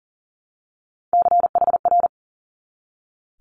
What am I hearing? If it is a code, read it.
Morse code, heard as a keyed tone, about 39 wpm, 700 Hz. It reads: CHR
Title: Multi-letter Phonemes 40wpm